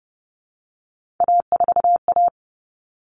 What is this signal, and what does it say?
Morse code, heard as a keyed tone, about 30 wpm, 700 Hz. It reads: A4A
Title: All Cut Numbers - 3-Digit 30wpm